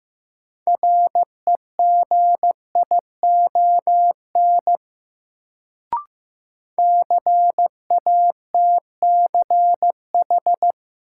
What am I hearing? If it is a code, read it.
Morse code, heard as a keyed tone, about 15 wpm, 700 Hz. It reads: REGION CATCH
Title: Sentences from Top 1500 Words - New Word Drill - 15wpm